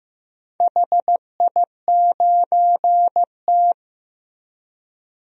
Morse code, heard as a keyed tone, about 15 wpm, 700 Hz. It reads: HI9T
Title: DXCC-100 Call Signs - 4 Characters - Repeated 2x - 15wpm